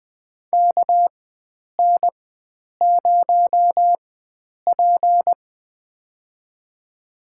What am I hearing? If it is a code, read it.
Morse code, heard as a keyed tone, about 20 wpm, spaced as for 9 wpm, 700 Hz. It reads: KN0P
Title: USA Call Signs - 4 Characters - 4x Character Spacing - 20wpm